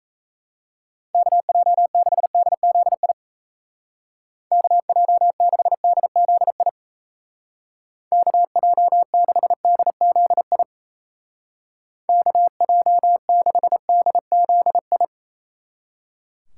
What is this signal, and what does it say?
Morse code, heard as a keyed tone, about 34 wpm, 700 Hz. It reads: KJ6DZI KJ6DZI KJ6DZI KJ6DZI
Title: Speed Racing - USA Call Signs 28wpm